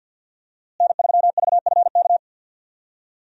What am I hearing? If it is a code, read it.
Morse code, heard as a keyed tone, about 53 wpm, 700 Hz. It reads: N3VFK
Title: USA Call Signs - 5 Characters - Repeated 2x - 50wpm